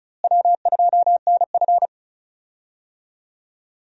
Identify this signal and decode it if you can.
Morse code, heard as a keyed tone, about 34 wpm, 700 Hz. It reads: W2NF
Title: USA Call Signs - 4 Characters - Repeated 3x - 35wpm